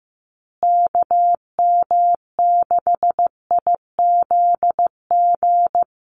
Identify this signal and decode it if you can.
Morse code, heard as a keyed tone, about 15 wpm, 700 Hz. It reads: KM6IZG
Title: USA Call Signs Repeated 3x 15wpm